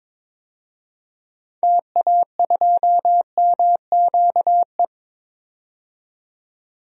Morse code, heard as a keyed tone, about 22 wpm, 700 Hz. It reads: TA2MQE
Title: DXCC-100 Call Signs - 6 Characters - 22wpm